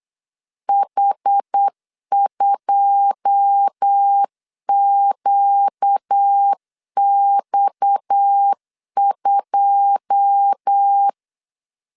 Morse code, heard as a keyed tone, about 8 wpm, 800 Hz. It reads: H2QX2
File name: LEZIONI / 9 2 8wpm 3437